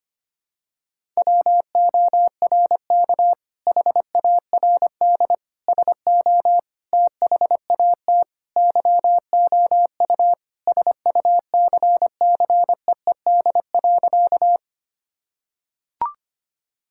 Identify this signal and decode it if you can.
Morse code, heard as a keyed tone, about 25 wpm, 700 Hz. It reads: WORK HARD SO THAT YOU SUCCEED.
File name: practice / 25wpm / Sentences with 2-Word Phrases - 25wpm